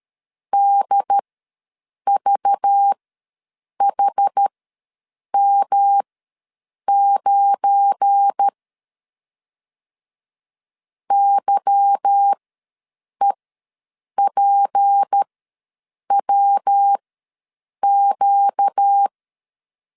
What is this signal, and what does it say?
Morse code, heard as a keyed tone, about 13 wpm, spaced as for 7 wpm, 800 Hz. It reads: DVHM9 YEPWQ
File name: LEZIONI / 8 YJ45 / 8 Y ws 12wpm 3409